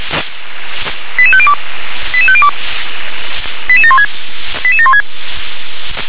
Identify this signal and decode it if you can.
Five-tone selective call (ZVEI), heard as quick runs of five tones, tones of about 70 ms. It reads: 88502 88502 88716 88716